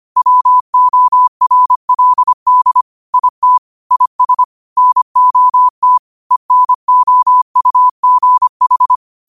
Morse code, heard as a keyed tone, about 25 wpm, 1000 Hz. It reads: WORLD IT IS NOT ENOUGH